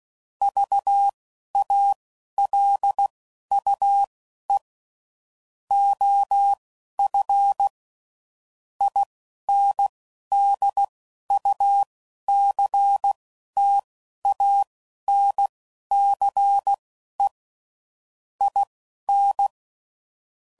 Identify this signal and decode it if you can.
Morse code, heard as a keyed tone, about 16 wpm, spaced as for 11 wpm, 800 Hz. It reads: VALUE OF INDUCTANCE IN
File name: CW Code Groups @10WPM / Technical Subjects / XL